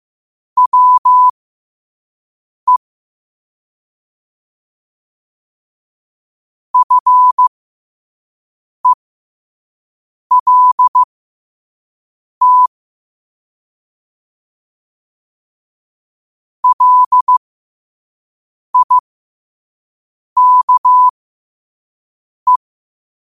Morse code, heard as a keyed tone, about 15 wpm, spaced as for 5 wpm, 1000 Hz. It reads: WE FELT LIKE
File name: qotdfiles / 5wpm